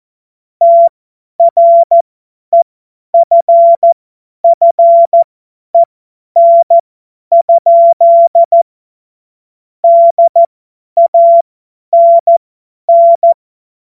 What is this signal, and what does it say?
Morse code, heard as a keyed tone, about 14 wpm, spaced as for 10 wpm, 680 Hz. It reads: TREFFEN? DANN
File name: Morsekurs / Witze / Witz 23. Dezember 2025 50 F70